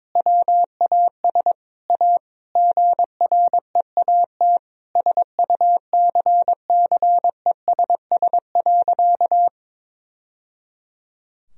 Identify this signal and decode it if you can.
Morse code, heard as a keyed tone, about 22 wpm, 700 Hz. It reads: WAS A GREAT SUCCESS.
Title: Sentences from Top 1000 Words - Review Part 4 - 22wpm